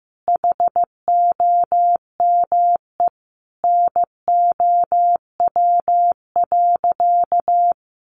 Morse code, heard as a keyed tone, about 15 wpm, 700 Hz. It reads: HOME NOW.
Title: Sentences from Top 1300 Words 15wpm